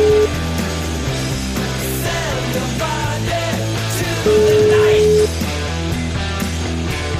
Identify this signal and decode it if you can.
Ringback tone (Europe, 425 Hz), with noise.